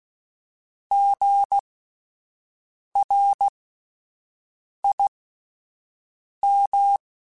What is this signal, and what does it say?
Morse code, heard as a keyed tone, about 16 wpm, spaced as for 5 wpm, 800 Hz. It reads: GRIM
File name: Spaced Alphas / Spaced Alphas mp3s / Gs sp @10